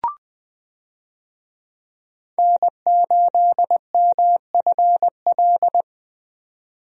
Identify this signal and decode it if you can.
Morse code, heard as a keyed tone, about 20 wpm, 700 Hz. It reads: N8MFL
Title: USA Call Signs - 5 Characters - 20wpm